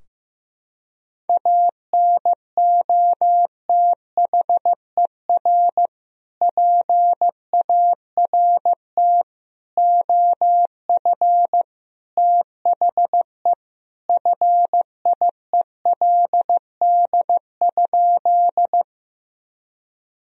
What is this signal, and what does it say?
Morse code, heard as a keyed tone, about 15 wpm, 700 Hz. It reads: ANOTHER PART OF THE FIELD?
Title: Sentences from Top 1000 Words - Review Part 2 - 15wpm